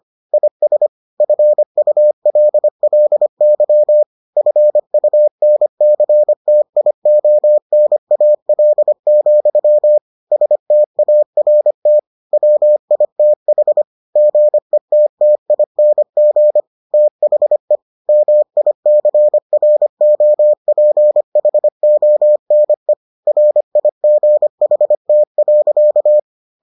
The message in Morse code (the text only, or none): IS FULLY FUNCTIONAL, START WITH GETTING THE MICROPHONE RIGHT.